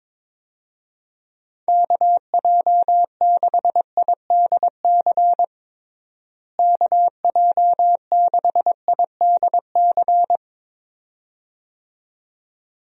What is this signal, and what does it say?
Morse code, heard as a keyed tone, about 22 wpm, 700 Hz. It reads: KJ6IDC KJ6IDC
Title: USA Call Signs - 6 Characters - Repeated 2x - 22wpm